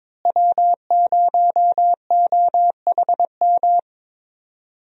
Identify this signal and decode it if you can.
Morse code, heard as a keyed tone, about 22 wpm, 700 Hz. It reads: W0OHM
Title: USA Call Signs - 5 Characters - Repeated 2x - 22wpm